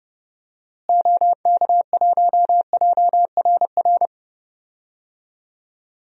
Morse code, heard as a keyed tone, about 30 wpm, 700 Hz. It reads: OK1JRR
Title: DXCC-100 Call Signs - 6 Characters - Repeated 3x - 30wpm